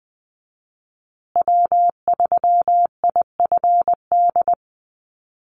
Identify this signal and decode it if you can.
Morse code, heard as a keyed tone, about 20 wpm, 700 Hz. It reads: W3IFD
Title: USA Call Signs Repeated 2x 20wpm